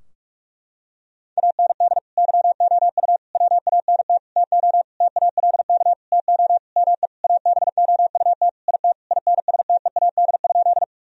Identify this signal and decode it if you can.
Morse code, heard as a keyed tone, about 45 wpm, 700 Hz. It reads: AND YOU WANT TO TALK TO ME ABOUT IT INSTEAD?
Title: Sentences from Top 1000 Words - Review Part 4 - 45wpm